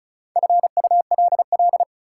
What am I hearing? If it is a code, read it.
Morse code, heard as a keyed tone, about 36 wpm, 700 Hz. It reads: FULL
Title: Sentences from Top 500 Words - New Word Drill - 35wpm